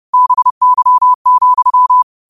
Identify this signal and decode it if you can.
Morse code, heard as a keyed tone, about 30 wpm, 1000 Hz. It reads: BY,
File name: qotdfiles / 30wpm